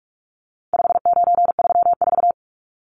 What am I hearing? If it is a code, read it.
Morse code, heard as a keyed tone, about 44 wpm, 700 Hz. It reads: 5934
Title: Numbers - 4-Digit 45wpm